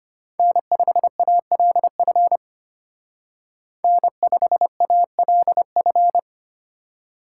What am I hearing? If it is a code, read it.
Morse code, heard as a keyed tone, about 32 wpm, 700 Hz. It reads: N5ALF N5ALF